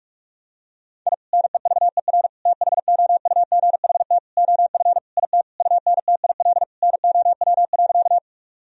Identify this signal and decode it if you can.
Morse code, heard as a keyed tone, about 46 wpm, 700 Hz. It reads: I NEVER THOUGHT OF IT UNTIL NOW.
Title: Sentences from Top 1000 Words - Review Part 1 - 45wpm